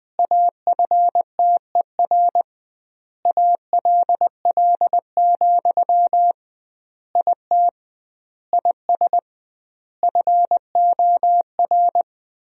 Morse code, heard as a keyed tone, about 20 wpm, 700 Hz. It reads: AFTER ALL, IT IS FOR
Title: 2x Word-Spacing - Sentences from Top 500 Words - 20wpm